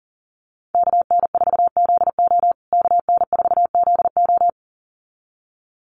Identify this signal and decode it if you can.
Morse code, heard as a keyed tone, about 37 wpm, 700 Hz. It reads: KN4ZO KN4ZO
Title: USA Call Signs Repeated 2x 40wpm